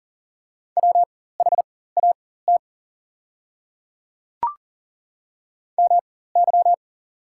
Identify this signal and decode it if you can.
Morse code, heard as a keyed tone, about 40 wpm, spaced as for 19 wpm, 700 Hz. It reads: WHAT MY
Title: 4x Character Spacing - Top 100 Words - 40wpm